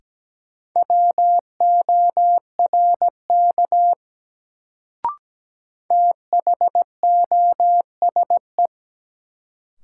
Morse code, heard as a keyed tone, about 17 wpm, 700 Hz. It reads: WORK THOSE
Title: Sets of 2 Words - New Word Drill - 17wpm